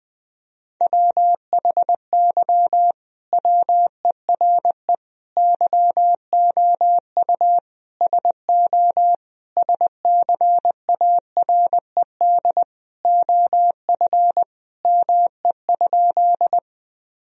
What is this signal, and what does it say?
Morse code, heard as a keyed tone, about 20 wpm, 700 Hz. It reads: WHY WERE YOU SO SCARED OF ME?
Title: Sentences from Top 1000 Words - Review Part 4 - 20wpm